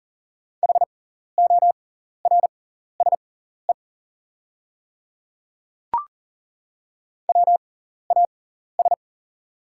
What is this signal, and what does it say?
Morse code, heard as a keyed tone, about 40 wpm, spaced as for 14 wpm, 700 Hz. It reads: HORSE WAS
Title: 6x Character Spacing - Top 300 Words - 40wpm